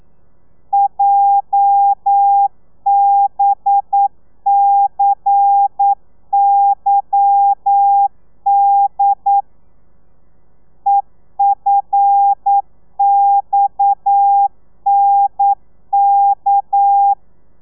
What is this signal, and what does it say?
Morse code, heard as a keyed tone, about 9 wpm, 800 Hz. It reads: JBCYD EFXNK
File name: telegrafia / i2tfj / Lez11 40